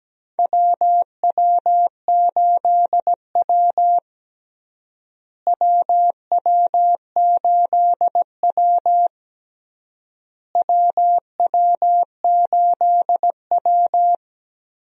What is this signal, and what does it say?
Morse code, heard as a keyed tone, about 17 wpm, 700 Hz. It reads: WW8W WW8W WW8W
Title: USA Call Signs - 4 Characters - Repeated 3x - 17wpm